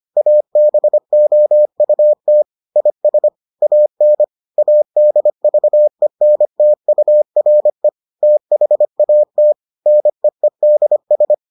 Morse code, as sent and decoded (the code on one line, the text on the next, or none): .- -... --- ..- - / .. ... / .- -. / .- -.. ...- . -. - ..- .-. / - .... .- - / -. . -.. ...
ABOUT IS AN ADVENTURE THAT NEEDS